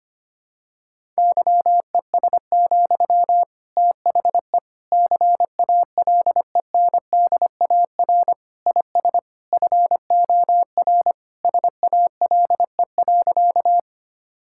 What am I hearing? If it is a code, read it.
Morse code, heard as a keyed tone, about 25 wpm, 700 Hz. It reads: YES, THE CALENDAR IS FOR SALE.
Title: Sentences from Top 1600 Words 25wpm